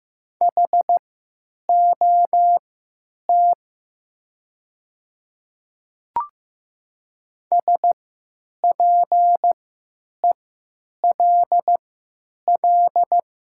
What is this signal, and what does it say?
Morse code, heard as a keyed tone, about 15 wpm, spaced as for 9 wpm, 700 Hz. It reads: HOT SPELL
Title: 3x Character Spacing - Top 200 Words - 15wpm